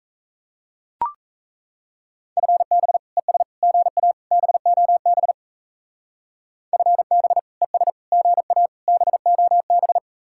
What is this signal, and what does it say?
Morse code, heard as a keyed tone, about 34 wpm, 700 Hz. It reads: FB ES GA BOB FB ES GA BOB
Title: Speed-Racing Common QSO Phrases Encore - 28wpm